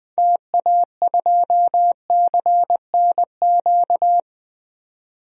Morse code, heard as a keyed tone, about 20 wpm, 700 Hz. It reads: TA2CNQ
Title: DXCC-100 Call Signs - 6 Characters - 20wpm